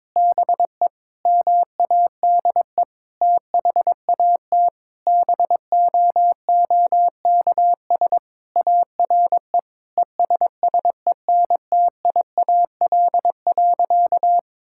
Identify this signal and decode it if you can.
Morse code, heard as a keyed tone, about 22 wpm, 700 Hz. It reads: BE MADE THAT BOOKS ARE ESSENTIAL.
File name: practice / 22wpm / Sentences with 8-Word Phrases - 22wpm